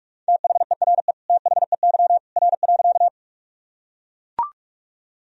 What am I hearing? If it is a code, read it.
Morse code, heard as a keyed tone, about 41 wpm, 700 Hz. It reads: THERE THEY R.
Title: Sentences from Top 100 Words and Abbreviations 45wpm